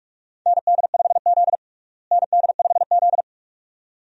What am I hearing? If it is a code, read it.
Morse code, heard as a keyed tone, about 42 wpm, 700 Hz. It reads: ND5Z ND5Z
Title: USA Call Signs - 4 Characters - Repeated 2x - 45wpm